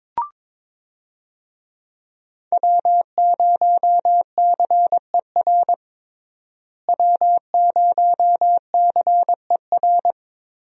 Morse code, heard as a keyed tone, about 22 wpm, 700 Hz. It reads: W0CER W0CER
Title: USA Call Signs - 5 Characters - Repeated 3x - 22wpm